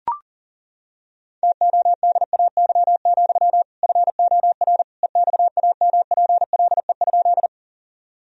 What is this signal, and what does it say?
Morse code, heard as a keyed tone, about 40 wpm, 700 Hz. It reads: TODAY, FOR EXAMPLE?
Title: Sentences from Top 1000 Words - Review Part 1 - 40wpm